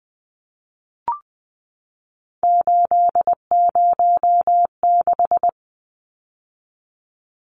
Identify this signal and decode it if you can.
Morse code, heard as a keyed tone, about 20 wpm, 700 Hz. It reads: 806
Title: Numbers - 3-Digit 20wpm